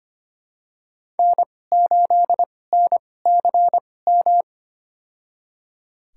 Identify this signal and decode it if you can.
Morse code, heard as a keyed tone, about 25 wpm, spaced as for 18 wpm, 700 Hz. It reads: N8NCM
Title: USA Call Signs - 5 Characters - 2x Character Spacing - 25wpm